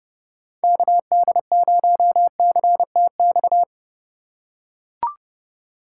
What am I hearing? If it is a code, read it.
Morse code, heard as a keyed tone, about 29 wpm, 700 Hz. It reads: KD0CTX